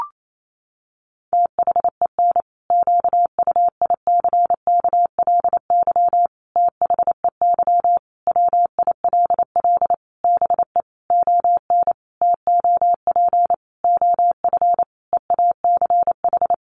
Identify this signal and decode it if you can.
Morse code, heard as a keyed tone, about 28 wpm, 700 Hz. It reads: THEN QUICKLY THEY WILL BE ON TOP OF EACH
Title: Sentences from Top 700 Words 28wpm